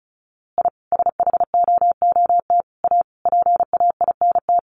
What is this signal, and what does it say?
Morse code, heard as a keyed tone, about 34 wpm, 700 Hz. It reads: I SHOOT A PAINT
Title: A Sound Of Thunder 35wpm